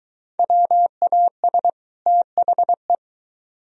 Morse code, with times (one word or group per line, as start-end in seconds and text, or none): 0.4-1.7 WAS
2.1-3.0 THE